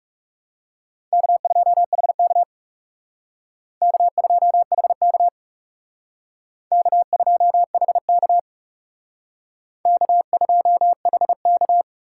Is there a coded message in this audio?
Yes: Morse code from 1.1 to 11.8 s, heard as a keyed tone, about 39 wpm, 700 Hz. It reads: K2HK K2HK K2HK K2HK